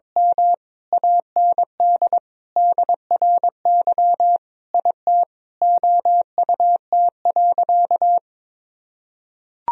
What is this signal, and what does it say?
Morse code, heard as a keyed tone, about 22 wpm, 700 Hz. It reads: M AND DRY IT OUT.
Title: Sentences from Top 1000 Words - Review Part 3 - 22wpm